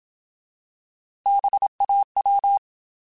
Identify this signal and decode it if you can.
Morse code, heard as a keyed tone, about 27 wpm, 800 Hz. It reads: BAW
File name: Alphabetical / Alphabetical mp3s / Bs @25